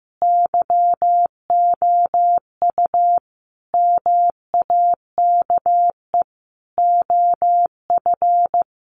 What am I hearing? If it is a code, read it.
Morse code, heard as a keyed tone, about 15 wpm, 700 Hz. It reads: YOU MAKE OF
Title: Sentences from Top 100 Words 15wpm